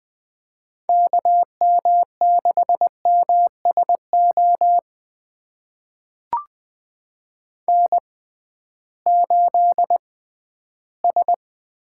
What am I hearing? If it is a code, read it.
Morse code, heard as a keyed tone, about 20 wpm, spaced as for 13 wpm, 700 Hz. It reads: KM6MSO N8S
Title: USA Call Signs - 6x Character Spacing - 20wpm